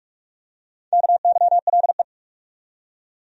Morse code, heard as a keyed tone, about 43 wpm, 700 Hz. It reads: KYLE
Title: USA Common Names - Max 4 Letters - 45wpm